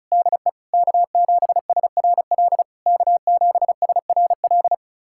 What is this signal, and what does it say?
Morse code, heard as a keyed tone, about 36 wpm, 690 Hz. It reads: DE K7SRL K7SRL
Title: SOTA QSO Practice 35wpm